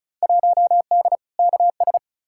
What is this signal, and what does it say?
Morse code, heard as a keyed tone, about 36 wpm, 700 Hz. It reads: 1D KS
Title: Field Day Exchanges - 35wpm